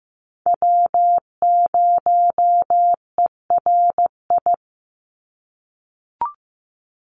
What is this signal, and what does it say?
Morse code, heard as a keyed tone, about 15 wpm, 700 Hz. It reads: W0ERI
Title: NA SOTA Chaser Callsigns 15wpm